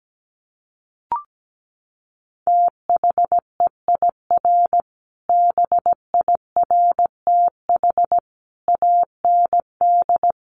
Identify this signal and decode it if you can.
Morse code, heard as a keyed tone, about 17 wpm, 700 Hz. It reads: THEIR BIRTH AND
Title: Sentences from Top 900 Words 17wpm